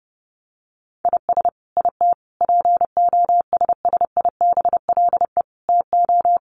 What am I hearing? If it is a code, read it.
Morse code, heard as a keyed tone, about 30 wpm, 700 Hz. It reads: IS IT POSSIBLE TO
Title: Sentences from Top 1200 Words 30wpm